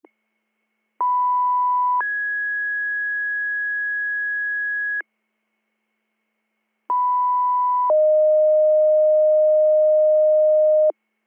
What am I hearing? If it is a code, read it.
Two-tone page, heard as a one-second tone, then a three-second tone, about 990 then 1650 Hz, about 990 then 620 Hz. It is heard from 1.0 to 10.9 s.